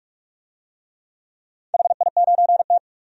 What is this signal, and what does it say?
Morse code, heard as a keyed tone, about 46 wpm, 700 Hz. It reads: HI9T
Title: DXCC-100 Call Signs - 4 Characters - 45wpm